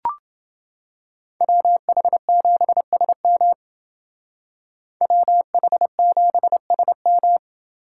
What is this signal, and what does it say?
Morse code, heard as a keyed tone, about 28 wpm, 700 Hz. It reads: WH7SM WH7SM